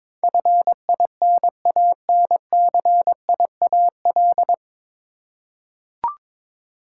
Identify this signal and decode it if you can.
Morse code, heard as a keyed tone, about 22 wpm, 700 Hz. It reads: FINANCIAL
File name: 9 Letter Words - First 3 Letters Given - 22wpm